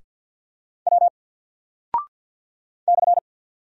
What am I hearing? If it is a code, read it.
Morse code, heard as a keyed tone, about 50 wpm, 700 Hz. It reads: W /
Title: Single Character (Letter:Number:Punctuation:Prosign) 50wpm